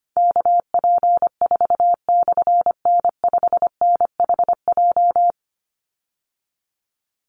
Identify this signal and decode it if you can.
Morse code, heard as a keyed tone, about 25 wpm, 700 Hz. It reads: KP4/N5NHJ
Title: Call Signs Repeated 3x - 25wpm